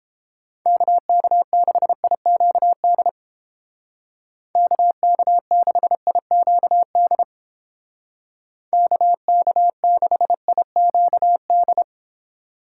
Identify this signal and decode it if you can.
Morse code, heard as a keyed tone, about 29 wpm, 700 Hz. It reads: KK6IQD KK6IQD KK6IQD